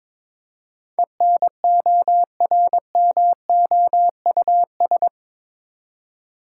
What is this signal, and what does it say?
Morse code, heard as a keyed tone, about 22 wpm, 700 Hz. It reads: ENORMOUS
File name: practice / 22wpm / Eight Letter Words - 22wpm